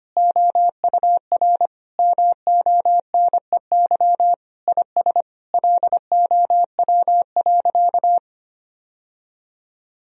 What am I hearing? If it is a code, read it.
Morse code, heard as a keyed tone, about 25 wpm, 700 Hz. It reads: OUR MONEY IS LOW.
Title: Sentences from Top 1000 Words - Review Part 1 - 25wpm